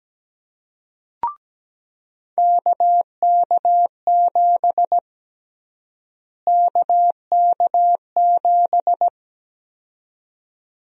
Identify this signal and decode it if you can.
Morse code, heard as a keyed tone, about 17 wpm, 700 Hz. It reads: KK7 KK7
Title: USA Call Sign Prefixes - Repeated 2x - 17wpm